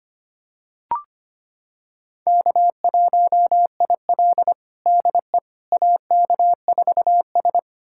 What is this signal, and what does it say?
Morse code, heard as a keyed tone, about 25 wpm, 700 Hz. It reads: K1IL DE AK4S
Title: USA Callsign DE Callsign - 4 Characters - 25wpm